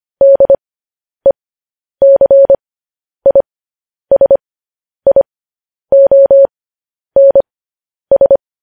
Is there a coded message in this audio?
Yes: Morse code from 0.2 to 8.4 s, heard as a keyed tone, about 25 wpm, spaced as for 10 wpm, 550 Hz. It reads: DECISIONS